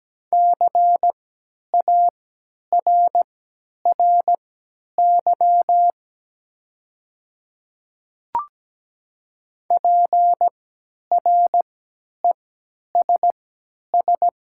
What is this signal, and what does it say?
Morse code, heard as a keyed tone, about 17 wpm, spaced as for 10 wpm, 700 Hz. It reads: CARRY PRESS
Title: 3x Character Spacing - Top 300 Words - 17wpm